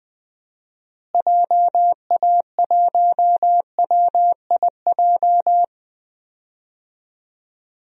Morse code, heard as a keyed tone, about 20 wpm, 700 Hz. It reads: JA1WIJ
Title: DXCC-100 Call Signs - 6 Characters - Repeated 2x - 20wpm